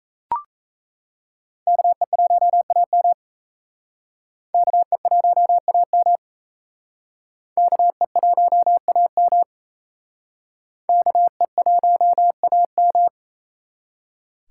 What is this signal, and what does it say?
Morse code, heard as a keyed tone, about 34 wpm, 700 Hz. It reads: KE1AM KE1AM KE1AM KE1AM
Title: Speed Racing - USA 2x2 Call Signs 28wpm